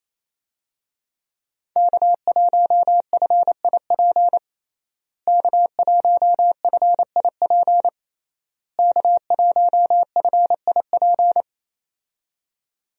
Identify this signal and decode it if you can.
Morse code, heard as a keyed tone, about 28 wpm, 700 Hz. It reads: K1FIP K1FIP K1FIP
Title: USA Call Signs - 5 Characters - Repeated 3x - 28wpm